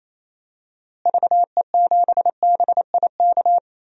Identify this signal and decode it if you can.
Morse code, heard as a keyed tone, about 27 wpm, 700 Hz. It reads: VE7BIK
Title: DXCC-100 Call Signs - 6 Characters - Repeated 3x - 28wpm